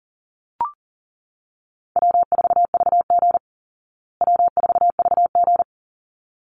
Morse code, heard as a keyed tone, about 40 wpm, 700 Hz. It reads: W4VG W4VG
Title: Easy Call Signs Repeated 2x - 40wpm